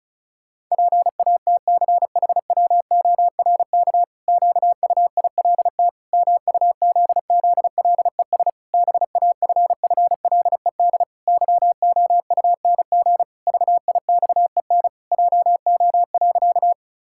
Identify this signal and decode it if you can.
Morse code, heard as a keyed tone, about 36 wpm, 700 Hz. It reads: PATCHWORK QUILT MUZZLES BAFFLED YOUNG VIXEN JO.
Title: Pangram Challenge v2 - 35wpm